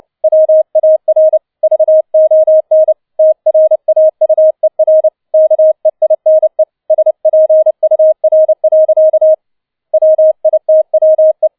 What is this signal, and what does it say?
Morse code, heard as a keyed tone, about 29 wpm, 630 Hz. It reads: WAR VON TRAUER KEINE SPUR. WITWE
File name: 007 Letzter Streich 30wpm